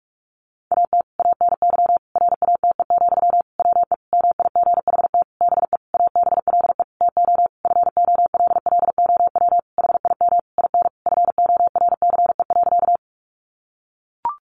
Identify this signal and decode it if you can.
Morse code, heard as a keyed tone, about 44 wpm, 700 Hz. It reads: AT ANY RATE, WE MIGHT BE ABLE TO FOLLOW HIM IN FORCE.
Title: Sentences from Top 700 Words 45wpm